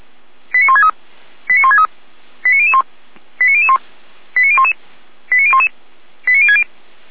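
Five-tone selective call (ZVEI), heard as quick runs of five tones, tones of about 70 ms. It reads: 89163 89163 89001 89001 89011 89011 89060
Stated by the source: ZVEI five-tone call 89119 received by SDR, from the feuerwehr-orlamuende.org audio archive.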